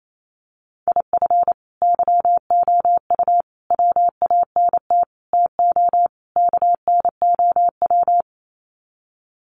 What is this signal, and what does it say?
Morse code, heard as a keyed tone, about 28 wpm, 700 Hz. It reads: IF YOU WANT TO KNOW
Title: Sets of 5 Words Encore2 28wpm